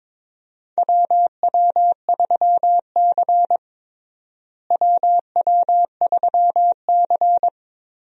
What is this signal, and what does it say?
Morse code, heard as a keyed tone, about 22 wpm, 700 Hz. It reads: WW3C WW3C
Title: USA Call Signs - 4 Characters - Repeated 3x - 22wpm